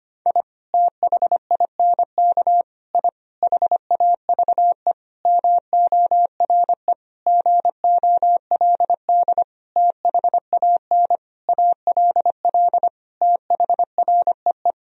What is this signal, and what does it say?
Morse code, heard as a keyed tone, about 25 wpm, 700 Hz. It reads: I THINK I HAVE MORE GOLD THAN ALL THREE
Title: Sentences from Top 1000 Words - Review Part 3 - 25wpm